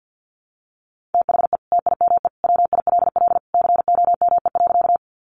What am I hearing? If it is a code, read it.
Morse code, heard as a keyed tone, about 50 wpm, 700 Hz. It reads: THE TIME WILL COME.
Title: Sentences from Top 100 Words 50wpm